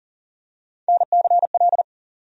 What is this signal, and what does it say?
Morse code, heard as a keyed tone, about 40 wpm, 700 Hz. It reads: NCL
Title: USA Call Sign Postfixes - 3 Letters 40wpm